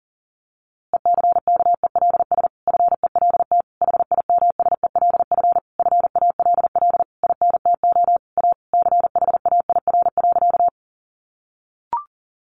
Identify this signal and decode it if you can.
Morse code, heard as a keyed tone, about 40 wpm, 700 Hz. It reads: ECKELS FELT HIMSELF FALL INTO A CHAIR.
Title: A Sound Of Thunder 40wpm